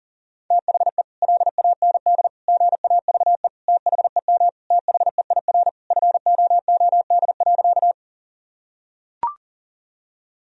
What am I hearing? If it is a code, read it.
Morse code, heard as a keyed tone, about 40 wpm, 700 Hz. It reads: THE LAND GAVE THEM THEIR FOOD.
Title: Sentences from Top 1000 Words - Review Part 1 - 40wpm